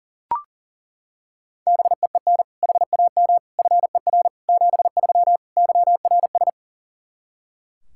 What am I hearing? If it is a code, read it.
Morse code, heard as a keyed tone, about 40 wpm, 700 Hz. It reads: BEEN HAM FER 73 YRS
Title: K1USN SST QSO Practice 40wpm